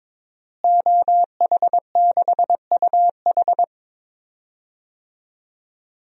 Morse code, heard as a keyed tone, about 22 wpm, 700 Hz. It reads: OH6UH
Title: DXCC-100 Call Signs - 5 Characters - Repeated 2x - 22wpm